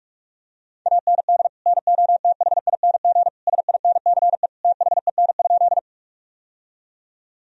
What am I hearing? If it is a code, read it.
Morse code, heard as a keyed tone, about 42 wpm, 700 Hz. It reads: AND NOTHING SINCE THEN?
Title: Sentences from Top 1000 Words - Review Part 1 - 45wpm